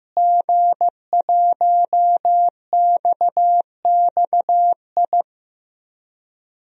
Morse code, heard as a keyed tone, about 16 wpm, 700 Hz. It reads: G1XXI
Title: DXCC-100 Call Signs - 5 Characters - Repeated 2x - 15wpm